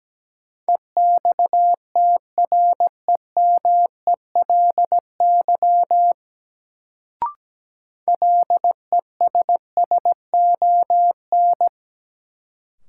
Morse code, heard as a keyed tone, about 17 wpm, 700 Hz. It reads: EXTREMELY LESSON
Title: Sentences from Top 1200 Words - New Word Drill Alt - 17wpm